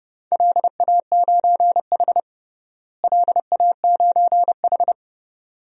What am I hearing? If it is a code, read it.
Morse code, heard as a keyed tone, about 30 wpm, 700 Hz. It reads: LA9H LA9H
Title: DXCC-100 Call Signs - 4 Characters - Repeated 3x - 30wpm